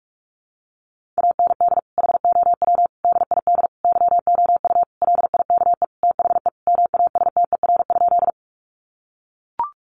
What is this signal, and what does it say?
Morse code, heard as a keyed tone, about 44 wpm, 700 Hz. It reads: AND HOW DID YOU LIKE THE MASTER?
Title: Sentences from Top 1300 Words 45wpm